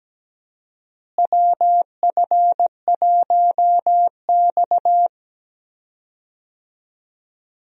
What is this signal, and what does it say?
Morse code, heard as a keyed tone, about 17 wpm, 700 Hz. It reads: WF1X
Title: USA Call Signs - 4 Characters - Repeated 2x - 17wpm